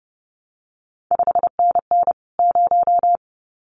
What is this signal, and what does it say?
Morse code, heard as a keyed tone, about 30 wpm, 700 Hz. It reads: 5NN 0
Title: CQ WW DX Contest Exchange Encore - 30wpm